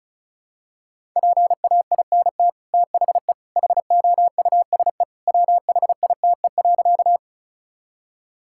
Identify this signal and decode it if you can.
Morse code, heard as a keyed tone, about 34 wpm, 700 Hz. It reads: PAINT THE HOUSE WHITE.
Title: Sentences from Top 1000 Words - Review Part 6 - 35wpm